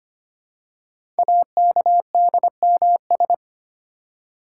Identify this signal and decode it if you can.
Morse code, heard as a keyed tone, about 25 wpm, 700 Hz. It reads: AKDMS
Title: Code Groups - 5 Letters - 25wpm